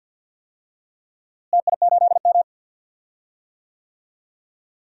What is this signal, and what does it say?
Morse code, heard as a keyed tone, about 51 wpm, 700 Hz. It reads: TI8M
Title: DXCC-100 Call Signs - 4 Characters - Repeated 3x - 50wpm